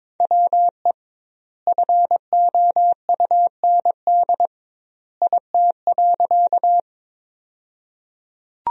Morse code, heard as a keyed tone, about 21 wpm, 700 Hz. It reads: WE FOUND IT.
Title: 2x Word-Spacing - Sentences from Top 300 Words - 22wpm